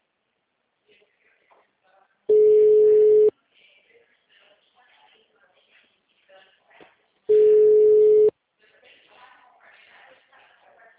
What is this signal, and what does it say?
Ringback tone (Europe, 425 Hz), with speech.
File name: out- -2001- - - .21579